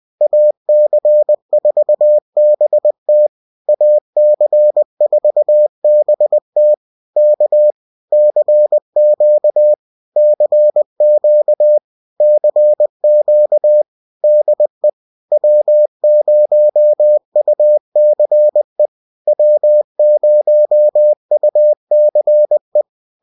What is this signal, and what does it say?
Morse code, heard as a keyed tone, about 20 wpm, 600 Hz. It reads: AC4BT AC4BT K CQ CQ CQ DE W0UCE W0UCE